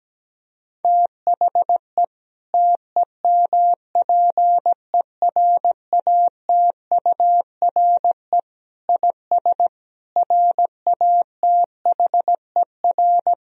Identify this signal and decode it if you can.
Morse code, heard as a keyed tone, about 17 wpm, 700 Hz. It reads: THE TEMPERATURE IS RATHER
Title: Sentences from Top 1000 Words - Review Part 4 - 17wpm